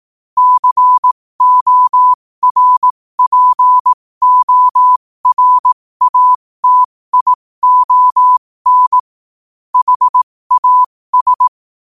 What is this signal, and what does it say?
Morse code, heard as a keyed tone, about 18 wpm, spaced as for 15 wpm, 1000 Hz. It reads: CORPORATION HAS
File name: 15wpm